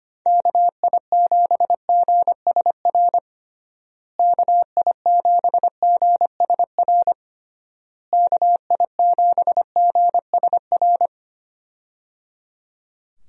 Morse code, heard as a keyed tone, about 25 wpm, 700 Hz. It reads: KI7GSR KI7GSR KI7GSR
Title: USA Call Signs - 6 Characters - Repeated 3x - 25wpm